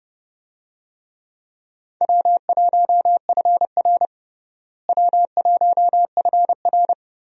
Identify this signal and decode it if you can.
Morse code, heard as a keyed tone, about 30 wpm, 700 Hz. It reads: W1FR W1FR
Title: USA Call Signs - 4 Characters - Repeated 3x - 30wpm